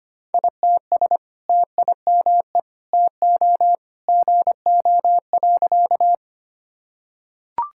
Morse code, heard as a keyed tone, about 25 wpm, 700 Hz. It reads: ITS TIME TO GO.
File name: practice / 25wpm / Sentences with 2-Word Phrases - 25wpm